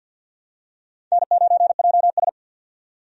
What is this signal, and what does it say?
Morse code, heard as a keyed tone, about 52 wpm, 700 Hz. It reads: N9JS
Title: USA Call Signs - 4 Characters - 50wpm